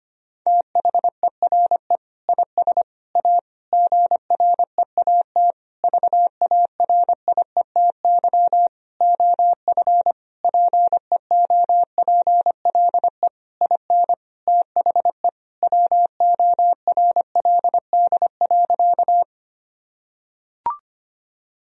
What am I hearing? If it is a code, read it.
Morse code, heard as a keyed tone, about 25 wpm, 700 Hz. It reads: THERE IS A GREAT VARIETY OF PEOPLE IN THE WORLD.
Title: Sentences from Top 1000 Words - Review Part 4 - 25wpm